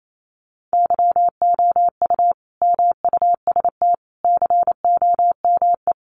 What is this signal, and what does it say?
Morse code, heard as a keyed tone, about 28 wpm, 700 Hz. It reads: YOU MUST COME
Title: Sentences from Top 200 Words 28wpm